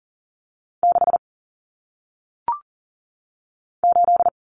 Morse code, heard as a keyed tone, about 39 wpm, 700 Hz. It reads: 6 8
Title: Numbers - 1-Digit 40wpm